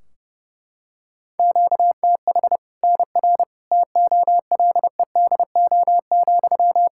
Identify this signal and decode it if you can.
Morse code, heard as a keyed tone, about 32 wpm, 700 Hz. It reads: QTH NR TOLEDO,
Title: Practice QSO - Exchange 1 - Beginning - 30wpm